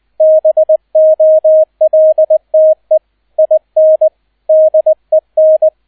Morse code, heard as a keyed tone, about 20 wpm, 630 Hz, with radio noise. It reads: BOLTE IN DEN
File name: 002 Zweiter Streich 20wpm